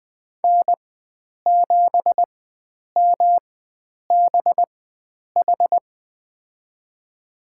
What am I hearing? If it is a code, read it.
Morse code, heard as a keyed tone, about 20 wpm, spaced as for 9 wpm, 700 Hz. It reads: N7MBH
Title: USA Call Signs - 5 Characters - 4x Character Spacing - 20wpm